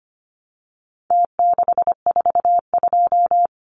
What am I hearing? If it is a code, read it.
Morse code, heard as a keyed tone, about 25 wpm, 700 Hz. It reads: T642
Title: Numbers - Contest Style - 4-Digit 25wpm